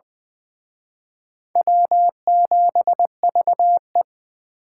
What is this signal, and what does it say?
Morse code, heard as a keyed tone, about 23 wpm, 700 Hz. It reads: W7VE
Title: Speed Racing - USA 1x2 Call Signs 20wpm